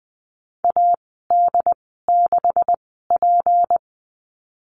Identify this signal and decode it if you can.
Morse code, heard as a keyed tone, about 20 wpm, spaced as for 14 wpm, 700 Hz. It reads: AD6P
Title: 161 Lesson 26g - Letter g - Call Signs 20wpm